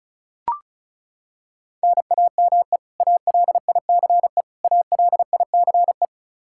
Morse code, heard as a keyed tone, about 36 wpm, 700 Hz. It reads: NAME ALICE ALICE
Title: Practice QSO - Exchange 1 - Beginning - 35wpm